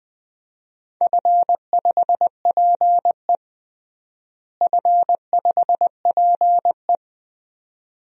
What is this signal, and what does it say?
Morse code, heard as a keyed tone, about 20 wpm, 700 Hz. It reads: F5PE F5PE
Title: DXCC-100 Call Signs - 4 Characters - Repeated 2x - 20wpm